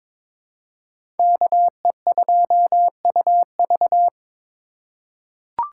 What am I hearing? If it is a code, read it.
Morse code, heard as a keyed tone, about 22 wpm, 700 Hz. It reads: KE2UV
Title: USA Call Signs - 5 Characters - 2x Character Spacing - 22wpm